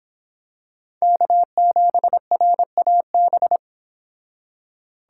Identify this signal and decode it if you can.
Morse code, heard as a keyed tone, about 26 wpm, 700 Hz. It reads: K7RAB